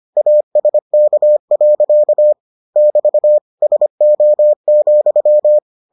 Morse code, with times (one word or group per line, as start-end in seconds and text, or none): 0.2-2.3 ASK.
2.8-3.4 <BT>
3.6-5.6 SO,